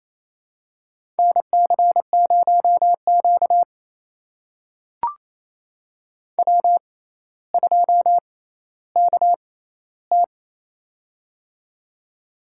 Morse code, heard as a keyed tone, about 28 wpm, spaced as for 14 wpm, 700 Hz. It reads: NC0Q W2KT
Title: USA Call Signs - 4 Characters - 6x Character Spacing - 28wpm